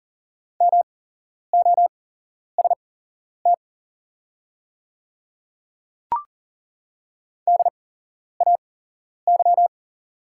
Morse code, heard as a keyed tone, about 40 wpm, spaced as for 11 wpm, 700 Hz. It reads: MOST DAY
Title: 8x Character Spacing - Top 100 Words - 40wpm